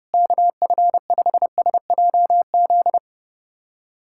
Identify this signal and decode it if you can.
Morse code, heard as a keyed tone, about 32 wpm, 700 Hz. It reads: KF5SJZ
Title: USA Callsign DE Callsign - 6 Characters - 30wpm